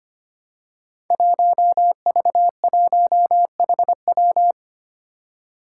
Morse code, heard as a keyed tone, about 25 wpm, 700 Hz. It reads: 1V1HW
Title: Five Challenging Letter-Number Combinations 25wpm